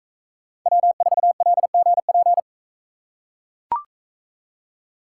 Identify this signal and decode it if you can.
Morse code, heard as a keyed tone, about 42 wpm, 700 Hz. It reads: W4LGP